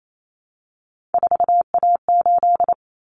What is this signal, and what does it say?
Morse code, heard as a keyed tone, about 28 wpm, 700 Hz. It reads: 4A8
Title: Common Cut Numbers - 3-Digit 28wpm